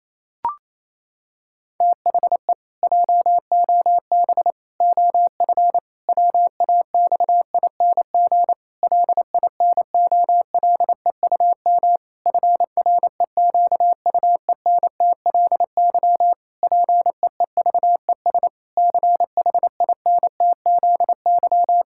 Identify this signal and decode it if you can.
Morse code, heard as a keyed tone, about 28 wpm, 700 Hz. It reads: THE JOB OF WAXING LINOLEUM FREQUENTLY PEEVES CHINTZY
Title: Pangram Challenge v2 - 28wpm